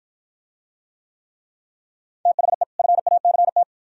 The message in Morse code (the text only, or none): THE FACT